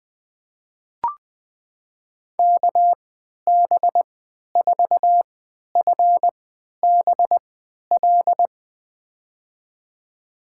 Morse code, heard as a keyed tone, about 21 wpm, spaced as for 11 wpm, 700 Hz. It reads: KB4FBL
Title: USA Call Signs - 3x Character Spacing - 20wpm